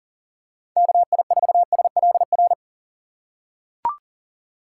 Morse code, heard as a keyed tone, about 40 wpm, 700 Hz. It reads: KI4SLR
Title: USA Call Signs - 6 Characters - 4x Character Spacing - 40wpm